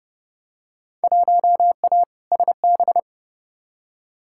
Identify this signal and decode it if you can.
Morse code, heard as a keyed tone, about 30 wpm, 700 Hz. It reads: 1A SB
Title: Field Day Exchanges Encore - 30wpm